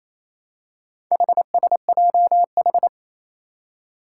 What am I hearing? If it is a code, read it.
Morse code, heard as a keyed tone, about 27 wpm, 700 Hz. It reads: HSJH
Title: Four Challenging Letter-Number Combinations 28wpm